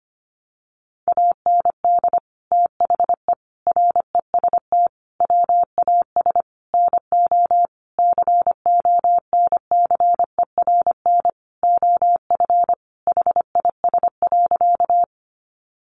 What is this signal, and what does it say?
Morse code, heard as a keyed tone, about 25 wpm, 700 Hz. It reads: AND THE REST WAS NO CONCERN OF HIS.
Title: Sentences from Top 1000 Words 25wpm